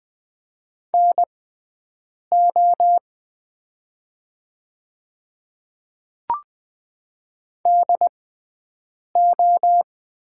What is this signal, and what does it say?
Morse code, heard as a keyed tone, about 20 wpm, spaced as for 7 wpm, 700 Hz. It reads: NO DO
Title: 6x Character Spacing - Top 300 Words - 20wpm